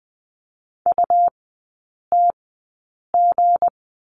Morse code, heard as a keyed tone, about 20 wpm, spaced as for 8 wpm, 700 Hz. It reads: UTG
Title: 184 Lesson 30e - Letter b - Three Characters 20wpm